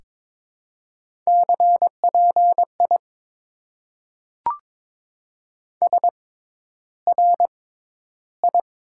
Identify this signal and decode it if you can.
Morse code, heard as a keyed tone, about 22 wpm, spaced as for 11 wpm, 700 Hz. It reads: CPI SRI
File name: Top 90 QSO Elements - 6x Character Spacing - 22wpm